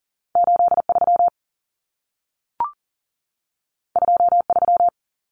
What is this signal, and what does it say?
Morse code, heard as a keyed tone, about 40 wpm, 700 Hz. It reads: 83 23
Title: Numbers - 2-Digit 40wpm